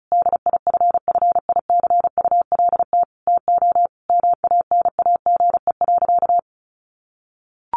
Morse code, heard as a keyed tone, about 34 wpm, 700 Hz. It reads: DIFFICULT TO MANAGE.
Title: Sentences from Top 1400 Words 35wpm